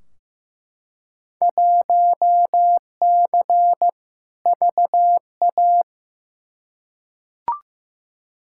Morse code, heard as a keyed tone, about 15 wpm, 700 Hz. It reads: 1C VA
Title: Field Day Exchanges Encore - 15wpm